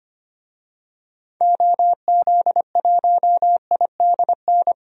Morse code, heard as a keyed tone, about 25 wpm, 700 Hz. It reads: OZ1IDN
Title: DXCC-100 Call Signs - 6 Characters - Repeated 3x - 25wpm